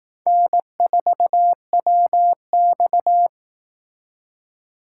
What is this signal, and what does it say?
Morse code, heard as a keyed tone, about 18 wpm, 700 Hz. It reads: N4WX
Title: Speed Racing - USA Call Signs 15wpm